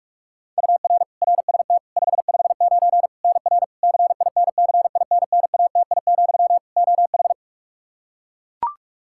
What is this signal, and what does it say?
Morse code, heard as a keyed tone, about 41 wpm, 700 Hz. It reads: UR RST 559 NR CINCINNATI, OH
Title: K1USN SST QSO Practice 45wpm